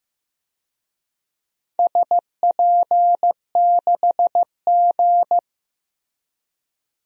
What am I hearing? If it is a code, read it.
Morse code, heard as a keyed tone, about 15 wpm, 700 Hz. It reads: SP6G
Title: DXCC-100 Call Signs - 4 Characters - Repeated 2x - 15wpm